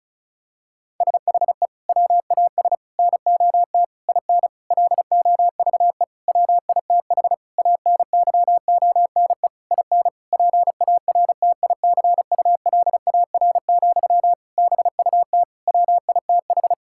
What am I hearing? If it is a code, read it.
Morse code, heard as a keyed tone, about 36 wpm, 700 Hz. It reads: SHE WAS NOT IN LOVE WITH ANYONE IN PARTICULAR, BUT WITH
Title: Sentences from Top 1000 Words - Review Part 5 - 35wpm